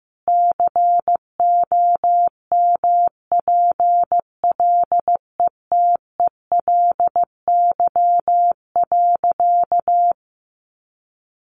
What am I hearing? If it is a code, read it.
Morse code, heard as a keyed tone, about 15 wpm, 700 Hz. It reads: COMPLETELY.
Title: Sentences from Top 1100 Words 15wpm